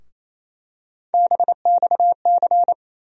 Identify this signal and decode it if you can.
Morse code, heard as a keyed tone, about 27 wpm, 700 Hz. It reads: BXC
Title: USA Call Sign Postfixes - 3 Letters 28wpm